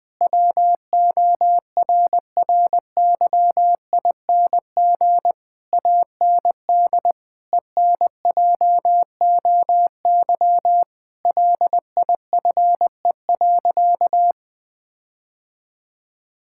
Morse code, heard as a keyed tone, about 20 wpm, 700 Hz. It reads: WORRYING AND ENJOY LIFE.
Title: Sentences with 8-Word Phrases - 20wpm